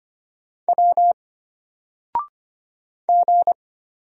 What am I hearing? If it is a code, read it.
Morse code, heard as a keyed tone, about 25 wpm, 700 Hz. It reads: W G
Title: Single Letter-Number 25wpm